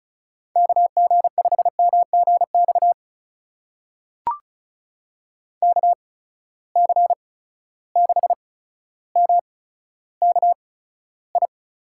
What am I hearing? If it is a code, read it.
Morse code, heard as a keyed tone, about 36 wpm, spaced as for 15 wpm, 700 Hz. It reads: KG5MGX KC6MKI
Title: USA Call Signs - 6 Characters - 8x Character Spacing - 35wpm